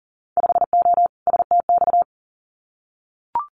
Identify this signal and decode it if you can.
Morse code, heard as a keyed tone, about 40 wpm, 700 Hz. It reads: 5O STX
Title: Winter Field Day Exchanges 40wpm